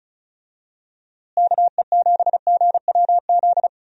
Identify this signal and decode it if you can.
Morse code, heard as a keyed tone, about 34 wpm, 700 Hz. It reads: KE7GWZ
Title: USA Call Signs - 6 Characters - Repeated 3x - 35wpm